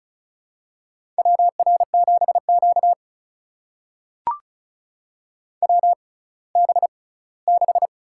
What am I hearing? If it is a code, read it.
Morse code, heard as a keyed tone, about 36 wpm, spaced as for 20 wpm, 700 Hz. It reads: WR7Q WB6
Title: USA Call Signs - 6x Character Spacing - 35wpm